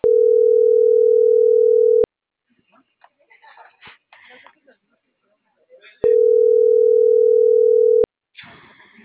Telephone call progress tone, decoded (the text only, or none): ringback tone